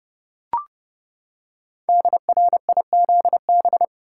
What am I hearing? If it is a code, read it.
Morse code, heard as a keyed tone, about 30 wpm, 700 Hz. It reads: DRIZB
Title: Code Groups - 5 Letters - 30wpm